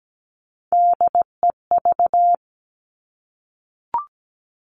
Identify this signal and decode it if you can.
Morse code, heard as a keyed tone, about 17 wpm, 700 Hz. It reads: DEV
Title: Three Challenging Letter-Number Combinations 17wpm